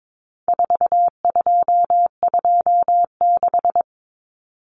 Morse code, heard as a keyed tone, about 22 wpm, 700 Hz. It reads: 4226
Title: Common Cut Numbers - 4-Digit 22wpm